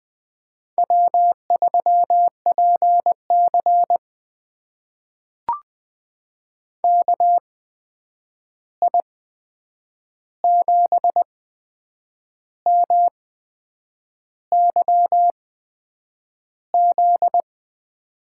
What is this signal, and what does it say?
Morse code, heard as a keyed tone, about 20 wpm, spaced as for 8 wpm, 700 Hz. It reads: W3PC KI7MYZ
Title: USA Call Signs - 8x Character Spacing - 20wpm